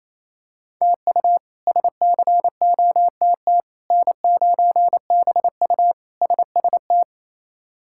Morse code, heard as a keyed tone, about 28 wpm, 700 Hz. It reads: TU SCOTT N9BU SST
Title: K1USN SST QSO Practice - 28wpm